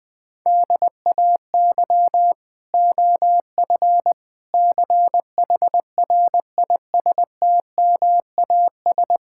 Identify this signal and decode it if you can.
Morse code, heard as a keyed tone, about 20 wpm, 700 Hz. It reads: DAY OF CHRISTMAS
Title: Sets of 9 Words - 20wpm